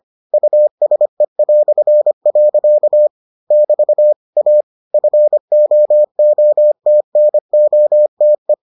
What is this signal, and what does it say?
Morse code, heard as a keyed tone, about 25 wpm, 600 Hz. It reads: USE". <BT> A FOOTNOTE